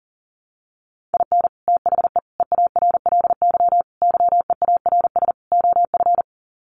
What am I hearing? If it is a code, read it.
Morse code, heard as a keyed tone, about 40 wpm, 700 Hz. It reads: IN THE EARLY YEARS OF
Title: Sets of 5 Words Encore 40wpm